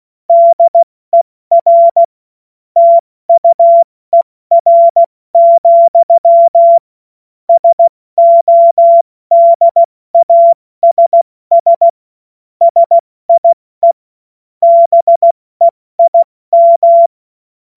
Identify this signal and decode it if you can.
Morse code, heard as a keyed tone, about 16 wpm, 680 Hz. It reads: DER TUER, SODASS SIE BEIM
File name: Morsekurs / Witze / Witz 13. Dezember 2022 70 F80